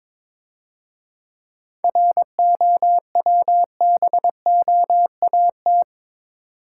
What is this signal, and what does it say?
Morse code, heard as a keyed tone, about 27 wpm, 690 Hz. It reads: ROWBOAT
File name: Easy Words - Speed-Racing with Hints - 22wpm